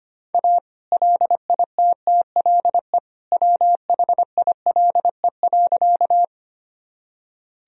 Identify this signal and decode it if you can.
Morse code, heard as a keyed tone, about 25 wpm, 700 Hz. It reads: A LITTLE WHILE.
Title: Sentences from Top 1000 Words - Review Part 1 - 25wpm